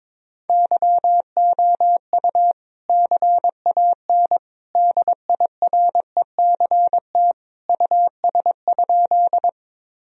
Morse code, heard as a keyed tone, about 22 wpm, 700 Hz. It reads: YOU CAN DIRECT US?
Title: Sentences from Top 1000 Words - Review Part 2 - 22wpm